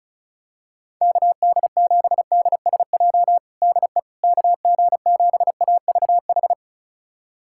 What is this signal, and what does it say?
Morse code, heard as a keyed tone, about 34 wpm, 700 Hz. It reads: KD7DSJ DE KG7AVH
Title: USA Callsign DE Callsign 35wpm